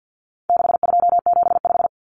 Morse code, heard as a keyed tone, about 48 wpm, 700 Hz. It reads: 6275
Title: Numbers - 4-Digit 50wpm